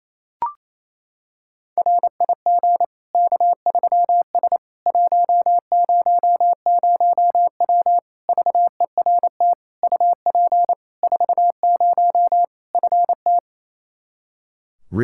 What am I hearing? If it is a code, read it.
Morse code, heard as a keyed tone, about 28 wpm, 700 Hz. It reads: RIG K3S 100W VERT UP 40 FT
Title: K1USN SST QSO Practice 28wpm